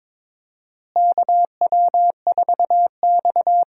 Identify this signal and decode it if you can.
Morse code, heard as a keyed tone, about 23 wpm, 700 Hz. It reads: KW4X
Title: USA Call Signs - 4 Characters - Repeated 2x - 22wpm